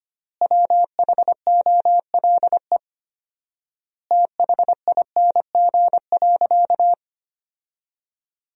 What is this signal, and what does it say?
Morse code, heard as a keyed tone, about 25 wpm, 700 Hz. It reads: WHOLE THING.
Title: 4x Word-Spacing - Sentences from Top 400 Words - 25wpm